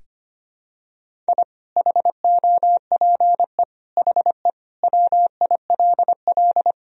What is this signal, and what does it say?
Morse code, heard as a keyed tone, about 25 wpm, 700 Hz. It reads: I HOPE HE WILL
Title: Sentences from Top 1000 Words - Review Part 1 - 25wpm